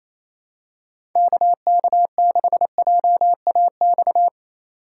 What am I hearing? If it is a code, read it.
Morse code, heard as a keyed tone, about 28 wpm, 700 Hz. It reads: KK6JAX
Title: USA Call Signs - 6 Characters - 3x Character Spacing - 28wpm